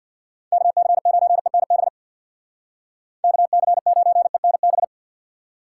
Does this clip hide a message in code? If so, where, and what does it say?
Yes: Morse code from 0.5 to 4.8 s, heard as a keyed tone, about 65 wpm, 700 Hz. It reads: KC9ENB KC9ENB